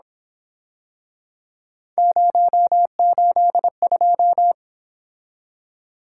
Morse code, heard as a keyed tone, about 27 wpm, 700 Hz. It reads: 082
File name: Speed-Racing - Three Letter-Number Combinations - 22wpm